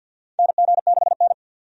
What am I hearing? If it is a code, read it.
Morse code, heard as a keyed tone, about 52 wpm, 700 Hz. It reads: NG6N
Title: USA Call Signs - 4 Characters - 50wpm